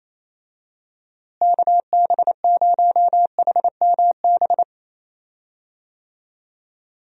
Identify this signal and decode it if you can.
Morse code, heard as a keyed tone, about 28 wpm, 700 Hz. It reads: KB0HMB
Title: USA Call Signs - 6 Characters - Repeated 2x - 28wpm